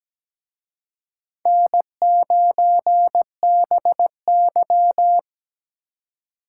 Morse code, heard as a keyed tone, about 20 wpm, 700 Hz. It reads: N9BY
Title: Speed Racing - USA 1x2 Call Signs 17wpm